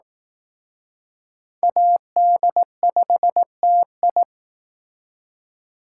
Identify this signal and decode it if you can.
Morse code, heard as a keyed tone, about 20 wpm, 700 Hz. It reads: AD5TI